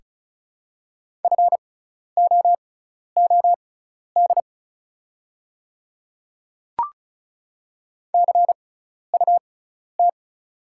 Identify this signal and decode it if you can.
Morse code, heard as a keyed tone, about 36 wpm, spaced as for 12 wpm, 690 Hz. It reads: FOOD CUT
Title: 6x Character Spacing - Top 500 Words - 35wpm